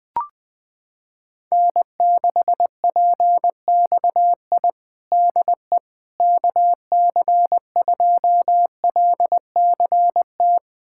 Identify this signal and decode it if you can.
Morse code, heard as a keyed tone, about 21 wpm, 700 Hz. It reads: N6PXI DE KC2LCT
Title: USA Callsign DE Callsign 20wpm